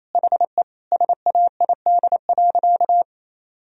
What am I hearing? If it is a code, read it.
Morse code, heard as a keyed tone, about 28 wpm, 700 Hz. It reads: HE SAID.
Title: Sentences from Top 1000 Words - Review Part 4 - 28wpm